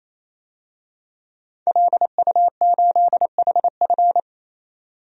Morse code, heard as a keyed tone, about 28 wpm, 700 Hz. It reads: LU8HF
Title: DXCC-100 Call Signs - 5 Characters - 28wpm